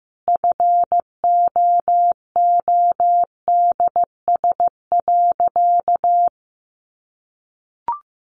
Morse code, heard as a keyed tone, about 15 wpm, 700 Hz. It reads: FOODS.
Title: Easy Phrases 15wpm